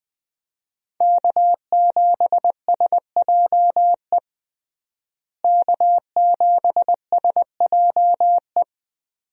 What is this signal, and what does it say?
Morse code, heard as a keyed tone, about 20 wpm, 700 Hz. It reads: K7SJE K7SJE
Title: USA Call Signs - 5 Characters - Repeated 2x - 20wpm